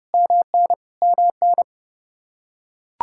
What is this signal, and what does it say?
Morse code, heard as a keyed tone, about 30 wpm, 700 Hz. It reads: MN MN
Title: Practice QSO - Exchange 1 - Beginning - 30wpm